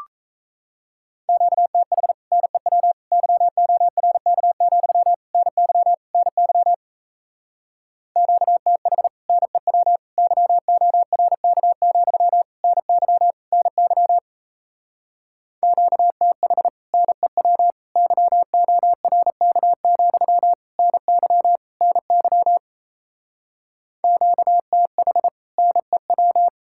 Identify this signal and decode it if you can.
Morse code, heard as a keyed tone, about 37 wpm, 700 Hz. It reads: QTH NEW YORK, NY NY QTH NEW YORK, NY NY QTH NEW YORK, NY NY QTH NEW